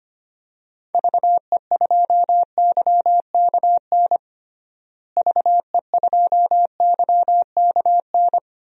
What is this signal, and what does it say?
Morse code, heard as a keyed tone, about 25 wpm, 700 Hz. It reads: VE2YKN VE2YKN
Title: DXCC-100 Call Signs - 6 Characters - Repeated 3x - 25wpm